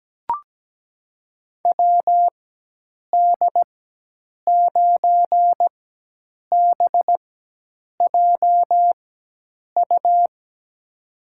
Morse code, heard as a keyed tone, about 17 wpm, spaced as for 8 wpm, 700 Hz. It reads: WD9BJU
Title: USA Call Signs - 6 Characters - 4x Character Spacing - 17wpm